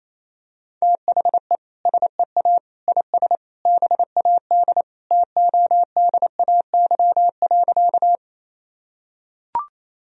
Morse code, heard as a keyed tone, about 28 wpm, 700 Hz. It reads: THE SEA IS BAD TODAY.
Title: Sentences from Top 1000 Words - Review Part 1 - 28wpm